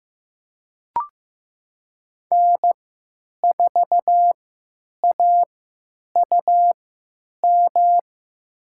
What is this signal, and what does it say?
Morse code, heard as a keyed tone, about 15 wpm, spaced as for 9 wpm, 700 Hz. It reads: N4AUM
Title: USA Call Signs - 5 Characters - 3x Character Spacing - 15wpm